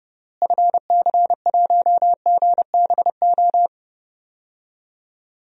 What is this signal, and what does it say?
Morse code, heard as a keyed tone, about 32 wpm, 700 Hz. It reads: FC1GBO
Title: DXCC-100 Call Signs - 6 Characters - Repeated 2x - 30wpm